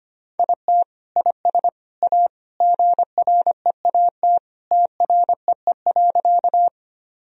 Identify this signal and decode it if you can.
Morse code, heard as a keyed tone, about 25 wpm, 700 Hz. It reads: IT IS A GREAT TREE.
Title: Sentences from Top 1000 Words - Review Part 1 - 25wpm